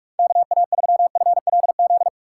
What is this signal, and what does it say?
Morse code, heard as a keyed tone, about 43 wpm, 700 Hz. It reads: KA3FLZ
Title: USA Call Signs - 4x Character Spacing - 45wpm